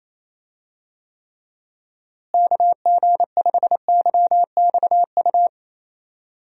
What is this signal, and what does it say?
Morse code, heard as a keyed tone, about 27 wpm, 700 Hz. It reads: KG5YXU
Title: USA Call Signs - 6 Characters - Repeated 3x - 28wpm